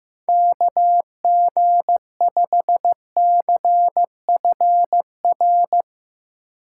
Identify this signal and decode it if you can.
Morse code, heard as a keyed tone, about 16 wpm, 700 Hz. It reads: KG5CFR
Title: Speed Racing - USA 2x3 Call Signs 15wpm